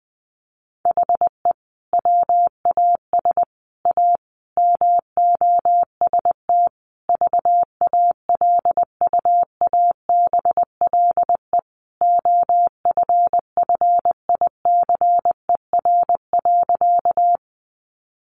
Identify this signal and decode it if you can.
Morse code, heard as a keyed tone, about 20 wpm, 700 Hz. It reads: HE WAS A MOST VALUABLE OFFICER.
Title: Sentences from Top 1200 Words 20wpm